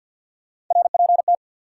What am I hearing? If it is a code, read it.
Morse code, heard as a keyed tone, about 48 wpm, 700 Hz. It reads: RPT
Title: Top 180 QSO Elements 50wpm